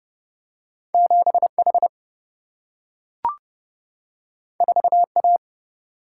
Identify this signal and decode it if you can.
Morse code, heard as a keyed tone, about 30 wpm, 700 Hz. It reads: 7 H 4 A
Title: Single-Letter Single-Number 30wpm